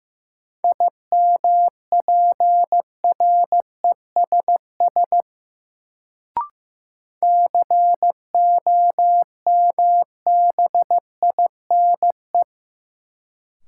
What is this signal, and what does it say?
Morse code, heard as a keyed tone, about 15 wpm, 700 Hz. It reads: IMPRESS COMBINE
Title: Sentences from Top 1600 Words - New Word Drill - 15wpm